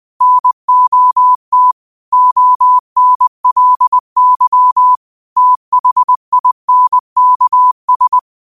Morse code, heard as a keyed tone, about 20 wpm, 1000 Hz. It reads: NOT ONLY THINKS
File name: qotdfiles / 20wpm